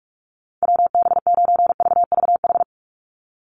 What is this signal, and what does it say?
Morse code, heard as a keyed tone, about 44 wpm, 700 Hz. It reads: WB9VVH
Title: USA Call Signs 45wpm